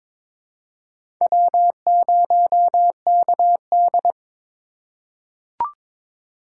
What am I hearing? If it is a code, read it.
Morse code, heard as a keyed tone, about 22 wpm, 700 Hz. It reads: W0KD
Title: USA Call Signs - 4 Characters - 2x Character Spacing - 22wpm